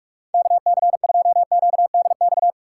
Morse code, heard as a keyed tone, about 44 wpm, 700 Hz. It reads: KC2QDX
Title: USA Call Signs - 6 Characters - 8x Character Spacing - 45wpm